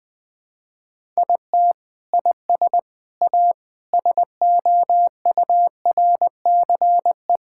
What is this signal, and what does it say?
Morse code, heard as a keyed tone, about 21 wpm, 700 Hz. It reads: IT IS A SOURCE
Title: Sentences from Top 1000 Words - Review Part 5 - 20wpm